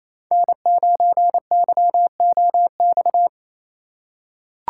Morse code, heard as a keyed tone, about 28 wpm, 700 Hz. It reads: N9YOX
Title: USA Call Signs - 5 Characters - 4x Character Spacing - 28wpm